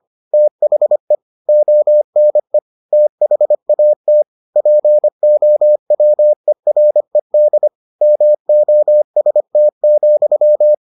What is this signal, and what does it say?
Morse code, heard as a keyed tone, about 25 wpm, 600 Hz. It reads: THE ONE THAT POWERED MOST,